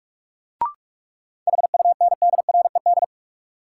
Morse code, heard as a keyed tone, about 46 wpm, 700 Hz. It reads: HUNDRED
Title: Sentences from Top 400 Words - New Word Drill - 45wpm